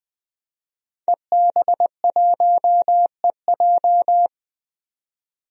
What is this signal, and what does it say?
Morse code, heard as a keyed tone, about 20 wpm, 700 Hz. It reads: EB1EJ
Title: DXCC-100 Call Signs - 5 Characters - 20wpm